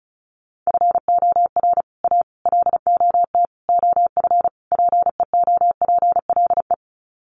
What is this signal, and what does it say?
Morse code, heard as a keyed tone, about 34 wpm, 700 Hz. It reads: FOR A LOT OF PEOPLE
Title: Sets of 5 Words Encore 35wpm